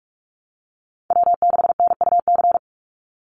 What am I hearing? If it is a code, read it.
Morse code, heard as a keyed tone, about 44 wpm, 700 Hz. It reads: W6NUC
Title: Call Signs Repeated 2x - 45wpm